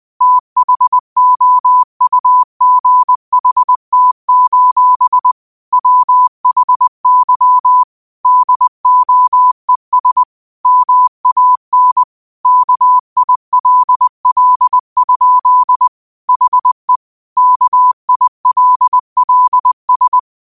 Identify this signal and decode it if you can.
Morse code, heard as a keyed tone, about 20 wpm, 1000 Hz. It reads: THOUGHT: WHY DOES MAN KILL? HE KILLS